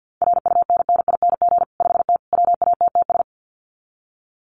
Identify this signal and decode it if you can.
Morse code, heard as a keyed tone, about 50 wpm, 700 Hz. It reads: RUNNING 5T WATTS
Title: Common QSO Phrases 50wpm